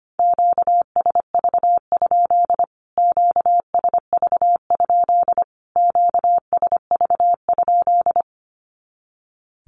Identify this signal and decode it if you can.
Morse code, heard as a keyed tone, about 25 wpm, 700 Hz. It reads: QSV? QSV? QSV?
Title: Q-Codes Repeated 3x 25wpm